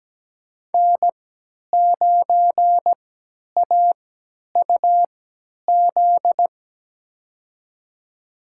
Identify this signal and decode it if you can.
Morse code, heard as a keyed tone, about 17 wpm, spaced as for 10 wpm, 700 Hz. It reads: N9AUZ
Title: USA Call Signs - 5 Characters - 3x Character Spacing - 17wpm